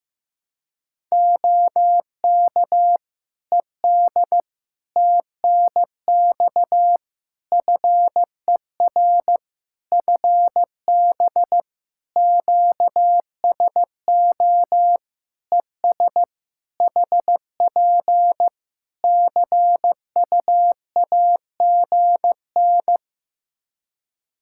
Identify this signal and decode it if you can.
Morse code, heard as a keyed tone, about 15 wpm, 700 Hz. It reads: OK ED TNX FER FB QSO ES HP CUAGN
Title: Practice QSO - Exchange 4 - Ending - 15wpm